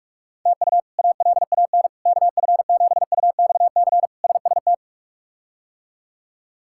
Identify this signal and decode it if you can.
Morse code, heard as a keyed tone, about 44 wpm, 700 Hz. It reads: TU ALAN KF7UXC SST
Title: K1USN SST QSO Practice - 45wpm